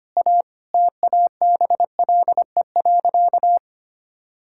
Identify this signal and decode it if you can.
Morse code, heard as a keyed tone, about 25 wpm, 700 Hz. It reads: A TABLE.
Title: Sentences from Top 1000 Words - Review Part 6 - 25wpm